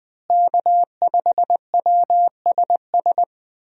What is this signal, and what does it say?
Morse code, heard as a keyed tone, about 20 wpm, 700 Hz. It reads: K5WSS
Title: USA Call Signs - 5 Characters - Repeated 2x - 20wpm